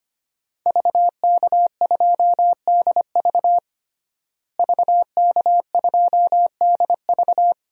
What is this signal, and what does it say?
Morse code, heard as a keyed tone, about 25 wpm, 700 Hz. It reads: VK2DV VK2DV
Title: DXCC-100 Call Signs - 5 Characters - Repeated 2x - 25wpm